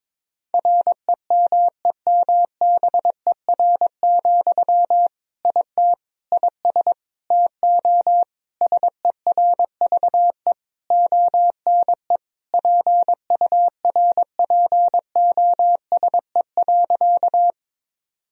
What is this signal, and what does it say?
Morse code, heard as a keyed tone, about 22 wpm, 700 Hz. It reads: REMEMBER, IT IS TO SERVE ONE PURPOSE.
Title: Sentences from Top 1000 Words - Review Part 2 - 22wpm